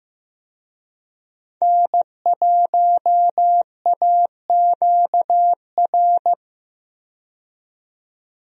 Morse code, heard as a keyed tone, about 15 wpm, 700 Hz. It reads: N1AQR
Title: USA Call Signs - 5 Characters - 15wpm